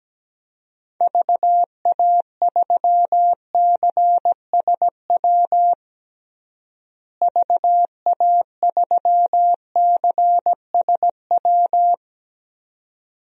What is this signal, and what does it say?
Morse code, heard as a keyed tone, about 17 wpm, 700 Hz. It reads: VA3CSW VA3CSW
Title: DXCC-100 Call Signs - 6 Characters - Repeated 2x - 17wpm